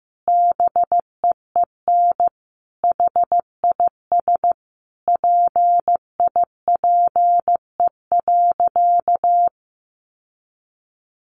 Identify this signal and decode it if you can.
Morse code, heard as a keyed tone, about 15 wpm, 700 Hz. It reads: BEEN HIS PIPE.
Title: Sentences from Top 1500 Words 15wpm